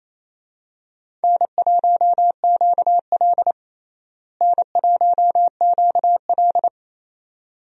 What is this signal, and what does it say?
Morse code, heard as a keyed tone, about 28 wpm, 700 Hz. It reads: N1QL N1QL
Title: USA Call Signs - 4 Characters - Repeated 2x - 28wpm